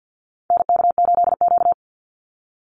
Morse code, heard as a keyed tone, about 50 wpm, 700 Hz. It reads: NK8Q
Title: NA SOTA Activator Callsigns 50wpm